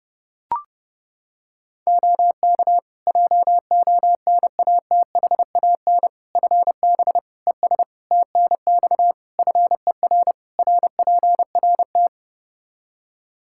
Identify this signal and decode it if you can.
Morse code, heard as a keyed tone, about 30 wpm, 700 Hz. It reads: OK JONATHAN FB ES TNX FER RPRT
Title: Practice QSO - Exchange 2 - Info - 30wpm